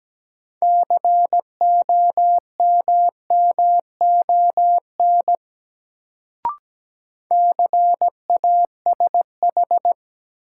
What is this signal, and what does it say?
Morse code, heard as a keyed tone, about 17 wpm, 700 Hz. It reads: COMMON CASH
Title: Sentences from Top 700 Words - New Word Drill - 17wpm